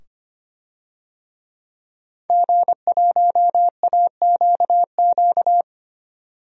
Morse code, heard as a keyed tone, about 25 wpm, 700 Hz. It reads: G1AQQ
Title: DXCC-100 Call Signs - 5 Characters - 25wpm